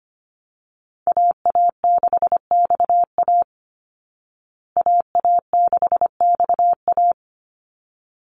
Morse code, heard as a keyed tone, about 25 wpm, 700 Hz. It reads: AA6XA AA6XA
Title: Call Signs Repeated 3x - 25wpm